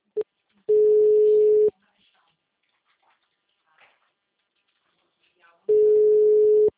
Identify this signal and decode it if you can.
Ringback tone (Europe, 425 Hz), with speech.
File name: out- -2014- - - .3